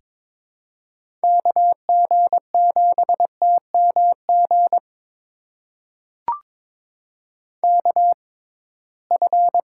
Morse code, heard as a keyed tone, about 23 wpm, spaced as for 17 wpm, 700 Hz. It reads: KG7TMG KF
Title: USA Call Signs - 6x Character Spacing - 22wpm